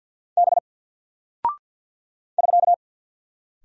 Morse code, heard as a keyed tone, about 50 wpm, 700 Hz. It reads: B <SK>
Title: Single Character (Letter:Number:Punctuation:Prosign) Alt 50wpm